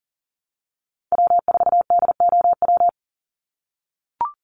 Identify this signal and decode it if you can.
Morse code, heard as a keyed tone, about 38 wpm, 700 Hz. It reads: W4DOW
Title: NA SOTA Chaser Callsigns 40wpm